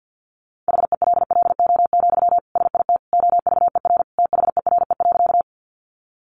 Morse code, heard as a keyed tone, about 50 wpm, 700 Hz. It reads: HELLO, SIT OVER THERE.
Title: Sentences from Top 1100 Words 50wpm